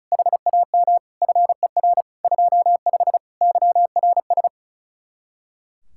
Morse code, heard as a keyed tone, about 45 wpm, 700 Hz. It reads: HAM FER 25 YRS